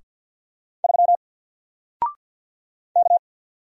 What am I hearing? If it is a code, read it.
Morse code, heard as a keyed tone, about 50 wpm, 700 Hz. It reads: 3 K
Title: Single Letter-Number Alt 50wpm